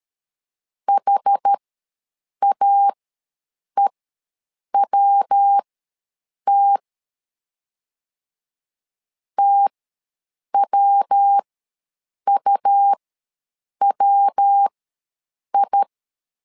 Morse code, heard as a keyed tone, about 13 wpm, spaced as for 7 wpm, 800 Hz. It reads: HAEWT TWUWI
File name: LEZIONI / 3 MWRO / 3 MW ws 12wpm 1816